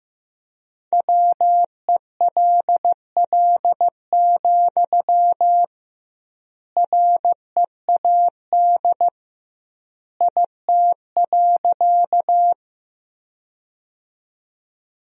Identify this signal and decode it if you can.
Morse code, heard as a keyed tone, about 15 wpm, 700 Hz. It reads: WELL, READ IT.
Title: 2x Word-Spacing - Sentences from Top 200 Words - 15wpm